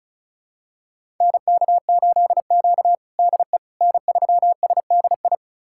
Morse code, heard as a keyed tone, about 34 wpm, 700 Hz. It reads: NK8Q DE N3SDI
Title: USA Callsign DE Callsign 35wpm